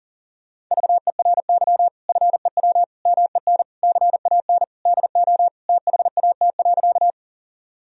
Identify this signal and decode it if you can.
Morse code, heard as a keyed tone, about 40 wpm, 700 Hz. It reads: VERY FEW MEN CAN DO THAT.
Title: Sentences from Top 1000 Words - Review Part 1 - 40wpm